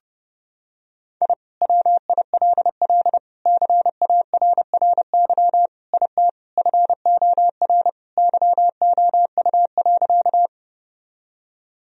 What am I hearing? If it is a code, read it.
Morse code, heard as a keyed tone, about 30 wpm, 700 Hz. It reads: I WILL CARRY IT FOR YOU.
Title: Sentences from Top 1000 Words - Review Part 1 - 30wpm